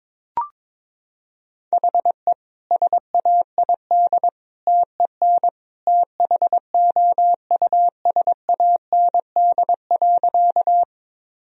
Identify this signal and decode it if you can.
Morse code, heard as a keyed tone, about 22 wpm, 690 Hz. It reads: HE SAID TEN THOUSAND.
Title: Sentences from Top 1000 Words - Review Part 3 - 22wpm